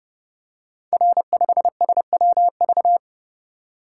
Morse code, heard as a keyed tone, about 28 wpm, 700 Hz. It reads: R5SWV
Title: Code Groups - 5 Letter:Numbers - 30wpm